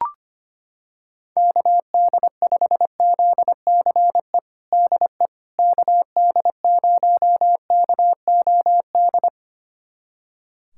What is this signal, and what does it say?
Morse code, heard as a keyed tone, about 25 wpm, 700 Hz. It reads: KD5ZCE DE KD0KOD
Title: USA Callsign DE Callsign - 6 Characters - 25wpm